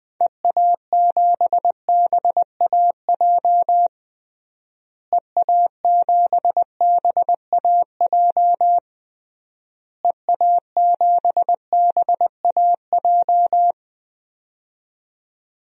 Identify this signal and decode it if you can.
Morse code, heard as a keyed tone, about 20 wpm, 700 Hz. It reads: EA7BAJ EA7BAJ EA7BAJ
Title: DXCC-100 Call Signs - 6 Characters - Repeated 3x - 20wpm